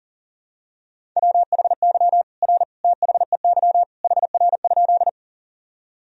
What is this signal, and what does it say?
Morse code, heard as a keyed tone, about 40 wpm, 700 Hz. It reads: WHY R THEY HR?
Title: Sentences from Top 200 Words and Abbreviations 40wpm